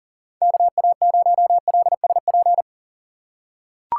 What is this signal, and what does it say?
Morse code, heard as a keyed tone, about 38 wpm, 700 Hz. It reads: KA0LSP
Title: USA Call Signs - 6x Character Spacing - 40wpm